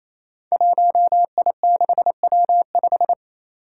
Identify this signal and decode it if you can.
Morse code, heard as a keyed tone, about 29 wpm, 700 Hz. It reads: 1I6W5
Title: Five Challenging Letter-Number Combinations 28wpm